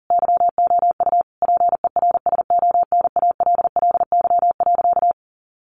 Morse code, heard as a keyed tone, about 40 wpm, 700 Hz. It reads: YOU PERSONALLY.
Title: Sentences from Top 1400 Words 40wpm